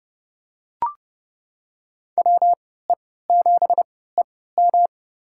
Morse code, heard as a keyed tone, about 31 wpm, spaced as for 17 wpm, 700 Hz. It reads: WE7EM
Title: USA Call Signs - 5 Characters - 3x Character Spacing - 30wpm